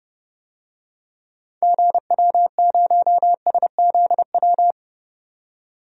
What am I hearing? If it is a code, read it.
Morse code, heard as a keyed tone, about 32 wpm, 700 Hz. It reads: GW0SZW
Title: DXCC-100 Call Signs - 6 Characters - 30wpm